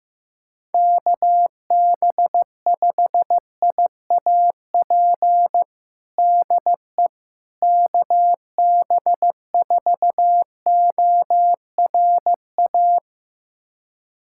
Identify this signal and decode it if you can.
Morse code, heard as a keyed tone, about 15 wpm, 700 Hz. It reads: KB5IAP DE KB4ORA
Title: USA Callsign DE Callsign - 6 Characters - 15wpm